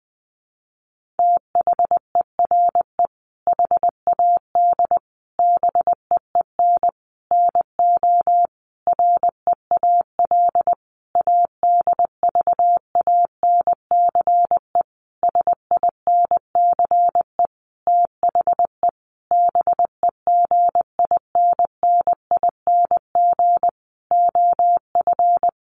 Morse code, heard as a keyed tone, about 20 wpm, 700 Hz. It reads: THERE HAD BEEN NO REAL ADVANCE SINCE THE BEGINNING OF
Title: Sentences from Top 1400 Words 20wpm